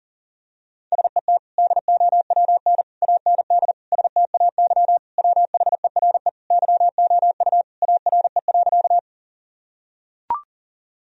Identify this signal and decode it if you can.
Morse code, heard as a keyed tone, about 40 wpm, 700 Hz. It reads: SET DOWN AND STAY WHERE YOU ARE.
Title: Sentences from Top 1000 Words - Review Part 1 - 40wpm